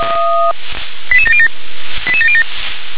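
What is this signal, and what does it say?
Five-tone selective call (ZVEI), heard as quick runs of five tones, tones of about 70 ms. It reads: 88707 88707
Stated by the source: Five-tone ZVEI selective call received on an SDR.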